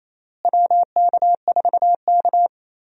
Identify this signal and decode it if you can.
Morse code, heard as a keyed tone, about 28 wpm, 700 Hz. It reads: WK4K
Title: USA Call Signs - 4 Characters - Repeated 3x - 28wpm